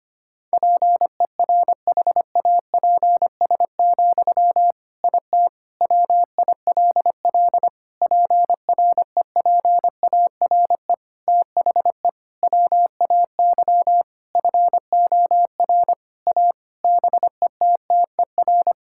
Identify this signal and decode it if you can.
Morse code, heard as a keyed tone, about 25 wpm, 700 Hz. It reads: PERHAPS, IT WILL PREPARE THE WAY FOR A BETTER
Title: Sentences from Top 1000 Words - Review Part 6 - 25wpm